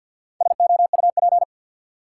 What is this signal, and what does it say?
Morse code, heard as a keyed tone, about 50 wpm, 700 Hz. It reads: SOUP
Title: Four Letter Words - 50wpm